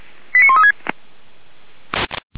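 Five-tone selective call (ZVEI), heard as quick runs of five tones, tones of about 70 ms. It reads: 89136
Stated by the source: ZVEI five-tone selective call received on an SDR, from the feuerwehr-orlamuende fire brigade audio archive.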